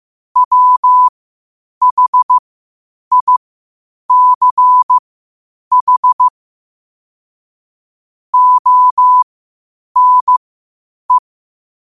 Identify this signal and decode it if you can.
Morse code, heard as a keyed tone, about 15 wpm, spaced as for 8 wpm, 1000 Hz. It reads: WHICH ONE
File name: qotdfiles / 8wpm